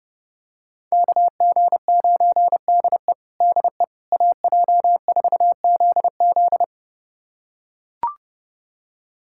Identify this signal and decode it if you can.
Morse code, heard as a keyed tone, about 32 wpm, 700 Hz. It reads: KG9DE DE AJ4ZZ
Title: USA Callsign DE Callsign - 5 Characters - 30wpm